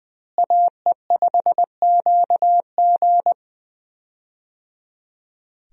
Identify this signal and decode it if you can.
Morse code, heard as a keyed tone, about 20 wpm, 700 Hz. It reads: AE5QG
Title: USA Call Signs - 5 Characters - Repeated 3x - 20wpm